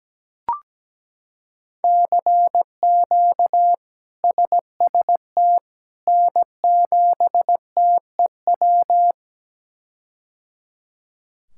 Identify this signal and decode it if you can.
Morse code, heard as a keyed tone, about 17 wpm, 700 Hz. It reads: CQ SST N7TEW
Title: K1USN SST QSO Practice - 17wpm